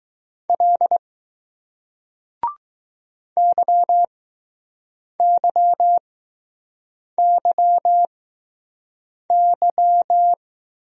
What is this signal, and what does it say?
Morse code, heard as a keyed tone, about 20 wpm, spaced as for 6 wpm, 700 Hz. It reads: L YYYY